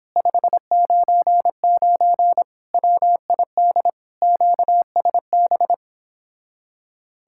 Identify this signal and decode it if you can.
Morse code, heard as a keyed tone, about 29 wpm, 700 Hz. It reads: 599 WID QSB